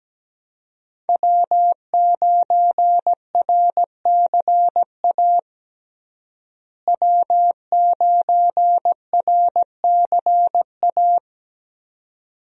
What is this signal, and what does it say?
Morse code, heard as a keyed tone, about 17 wpm, 700 Hz. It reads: W9RCA W9RCA
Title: USA Call Signs - 5 Characters - Repeated 2x - 17wpm